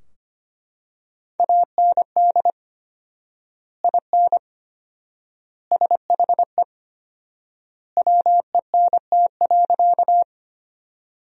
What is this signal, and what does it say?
Morse code, heard as a keyed tone, about 25 wpm, 700 Hz. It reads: AND IN SHE WENT.
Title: 4x Word-Spacing - Sentences from Top 200 Words - 25wpm